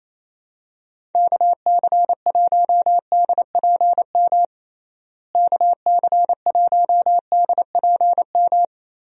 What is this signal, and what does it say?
Morse code, heard as a keyed tone, about 28 wpm, 700 Hz. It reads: KC1DPM KC1DPM
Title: USA Call Signs - 6 Characters - Repeated 3x - 28wpm